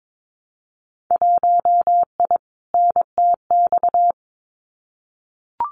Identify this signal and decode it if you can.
Morse code, heard as a keyed tone, about 22 wpm, 700 Hz. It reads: 1I NTX
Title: Winter Field Day Exchanges 22wpm